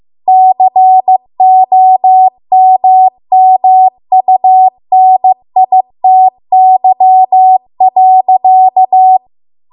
Morse code, heard as a keyed tone, about 15 wpm, 750 Hz. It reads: COMMUNITY.